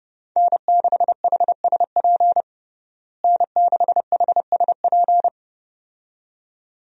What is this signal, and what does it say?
Morse code, heard as a keyed tone, about 32 wpm, 700 Hz. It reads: N6HSP N6HSP
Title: USA Call Signs - 5 Characters - Repeated 2x - 30wpm